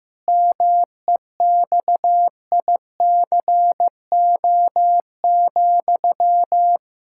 Morse code, heard as a keyed tone, about 15 wpm, 700 Hz. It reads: MEXICO,
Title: Pangram Challenge v2 - 15wpm